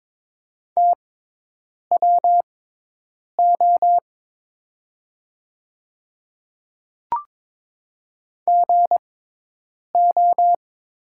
Morse code, heard as a keyed tone, about 21 wpm, spaced as for 8 wpm, 700 Hz. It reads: TWO GO
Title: 6x Character Spacing - Top 100 Words - 22wpm